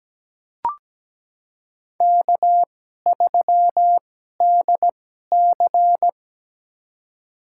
Morse code, heard as a keyed tone, about 17 wpm, spaced as for 12 wpm, 700 Hz. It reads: K3DC
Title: USA Call Signs - 4 Characters - 2x Character Spacing - 17wpm